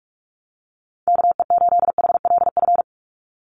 Morse code, heard as a keyed tone, about 44 wpm, 700 Hz. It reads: KE8HLF
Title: USA Call Signs 45wpm